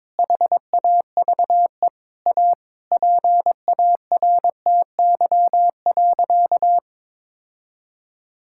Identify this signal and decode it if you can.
Morse code, heard as a keyed tone, about 22 wpm, 700 Hz. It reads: HAVE A PARTY.
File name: practice / 22wpm / Sentences with 4-Word Phrases - 22wpm